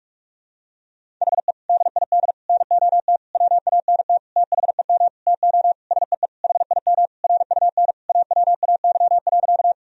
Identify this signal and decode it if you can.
Morse code, heard as a keyed tone, about 46 wpm, 700 Hz. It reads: HE DID NOT WANT THEM TO SEE HIM RUN AWAY.
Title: Sentences from Top 1000 Words - Review Part 1 - 45wpm